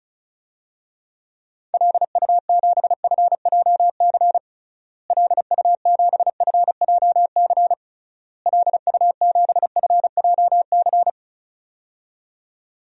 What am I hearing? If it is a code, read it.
Morse code, heard as a keyed tone, about 36 wpm, 700 Hz. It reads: LU7FJC LU7FJC LU7FJC
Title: DXCC-100 Call Signs - 6 Characters - Repeated 3x - 35wpm